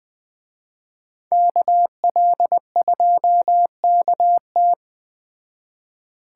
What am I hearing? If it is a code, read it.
Morse code, heard as a keyed tone, about 20 wpm, 700 Hz. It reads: KL2KT
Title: Speed Racing - USA Call Signs 17wpm